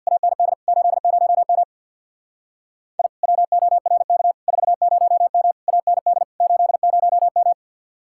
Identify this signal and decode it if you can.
Morse code, heard as a keyed tone, about 61 wpm, 700 Hz. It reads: AND 80M I WORK 40M AND 80M
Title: Speed-Racing Common QSO Phrases Encore - 50wpm